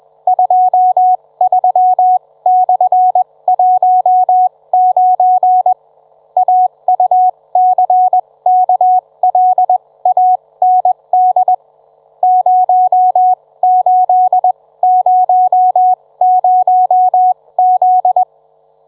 Morse code, heard as a keyed tone, about 21 wpm, spaced as for 17 wpm, 740 Hz. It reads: 23/19 AUCKLAND 0800Z